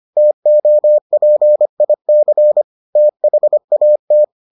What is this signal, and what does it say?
Morse code, heard as a keyed tone, about 25 wpm, 600 Hz. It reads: TOPIC THAT